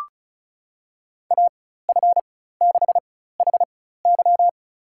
Morse code, heard as a keyed tone, about 36 wpm, spaced as for 16 wpm, 700 Hz. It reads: AF6HY
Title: USA Call Signs - 4x Character Spacing - 35wpm